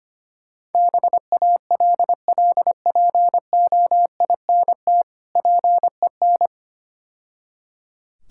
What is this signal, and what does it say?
Morse code, heard as a keyed tone, about 26 wpm, 700 Hz. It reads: BALLPOINT PEN
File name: Easy Words - Speed-Racing with Hints - 25wpm